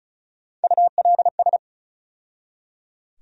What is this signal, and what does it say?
Morse code, heard as a keyed tone, about 34 wpm, 700 Hz. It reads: ULS
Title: USA Call Sign Postfixes - 3 Letters 35wpm